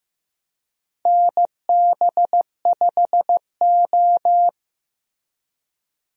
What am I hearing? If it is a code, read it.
Morse code, heard as a keyed tone, about 15 wpm, 700 Hz. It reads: NB5O
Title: USA Call Signs - 4 Characters - Repeated 3x - 15wpm